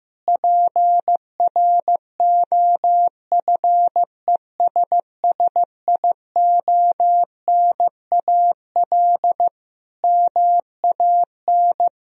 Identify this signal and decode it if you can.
Morse code, heard as a keyed tone, about 15 wpm, 700 Hz. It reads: PROFESSIONAL MAN
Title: Sentences from Top 1000 Words - Review Part 4 - 15wpm